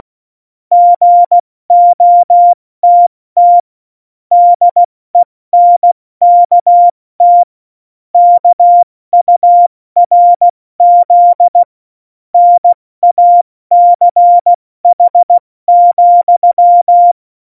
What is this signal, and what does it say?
Morse code, heard as a keyed tone, about 16 wpm, 700 Hz. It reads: GOTT DENKT KURZ NACH,
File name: Morsekurs / Witze / Witz 26. Januar 2021, 70 F80